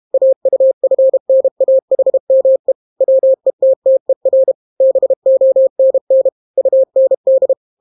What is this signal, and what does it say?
Morse code, heard as a keyed tone, about 32 wpm, 530 Hz. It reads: AUFNAHME WETTER BONN UND